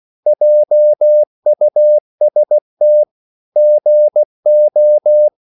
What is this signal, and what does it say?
Morse code, heard as a keyed tone, about 16 wpm, 600 Hz. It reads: JUST GO
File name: CW / Practice / EnglishQuotations0116